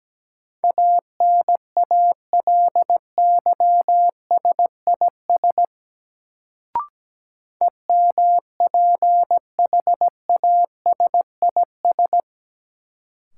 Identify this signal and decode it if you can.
Morse code, heard as a keyed tone, about 17 wpm, 700 Hz. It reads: ANALYSIS EMPHASIS
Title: Sentences from Top 1400 Words - New Word Drill Alt - 17wpm